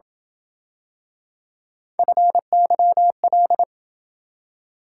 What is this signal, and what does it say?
Morse code, heard as a keyed tone, about 28 wpm, 700 Hz. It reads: FYL